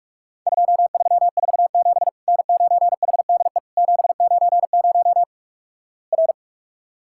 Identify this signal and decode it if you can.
Morse code, heard as a keyed tone, about 46 wpm, 700 Hz. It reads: 2347 N9HDE 790 R
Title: Worked All Europe DX Contest - QTC Exchange - 45wpm